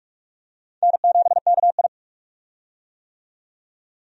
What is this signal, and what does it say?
Morse code, heard as a keyed tone, about 46 wpm, 700 Hz. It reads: N7KI
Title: USA Call Signs - 4 Characters - 45wpm